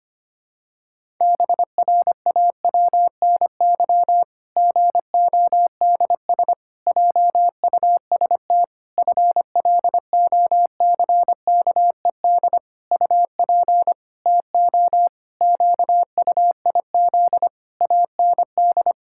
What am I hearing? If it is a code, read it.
Morse code, heard as a keyed tone, about 25 wpm, 700 Hz. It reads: BRAWNY GODS JUST FLOCKED UP TO QUIZ AND
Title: Pangram Challenge v2 - 25wpm